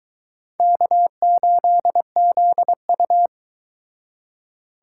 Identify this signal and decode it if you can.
Morse code, heard as a keyed tone, about 22 wpm, 700 Hz. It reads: K8ZU